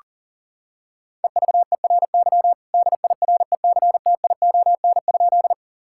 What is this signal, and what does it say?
Morse code, heard as a keyed tone, about 40 wpm, 700 Hz. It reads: EVERY DIRECTION?
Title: Sentences from Top 1000 Words - Review Part 5 - 40wpm